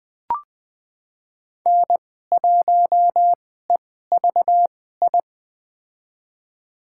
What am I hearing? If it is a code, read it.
Morse code, heard as a keyed tone, about 20 wpm, spaced as for 15 wpm, 700 Hz. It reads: N1EVI
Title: USA Call Signs - 5 Characters - 2x Character Spacing - 20wpm